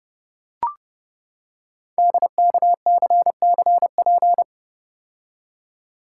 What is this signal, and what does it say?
Morse code, heard as a keyed tone, about 30 wpm, 700 Hz. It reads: DKCCP
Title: Code Groups - 5 Letters - 30wpm